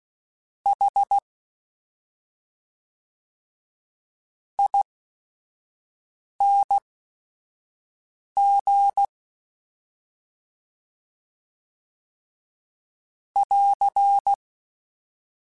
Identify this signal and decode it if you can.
Morse code, heard as a keyed tone, about 16 wpm, spaced as for 5 wpm, 800 Hz. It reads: H ING <AR>
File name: Special INGs / Special INGs-mp3s / SlowToFaster ing @5